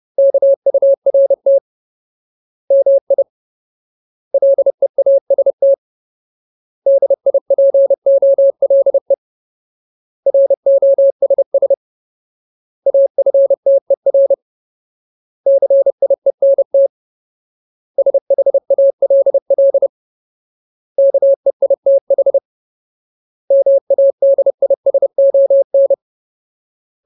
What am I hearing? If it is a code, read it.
Morse code, heard as a keyed tone, about 30 wpm, 550 Hz. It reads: KURT MI LEAST DIPOLE ROSS AFTER CIENT SHALL KEITH MADISON
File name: Ultimate Word List / 30WPM Ultimate